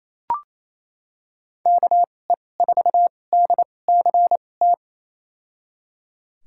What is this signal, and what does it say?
Morse code, heard as a keyed tone, about 28 wpm, spaced as for 20 wpm, 700 Hz. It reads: KE4DCT
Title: USA Call Signs - 6 Characters - 2x Character Spacing - 28wpm